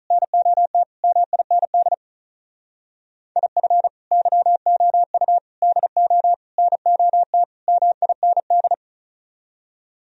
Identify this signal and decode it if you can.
Morse code, heard as a keyed tone, about 44 wpm, 700 Hz. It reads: NOT MIND IF YOU DO NOT MIND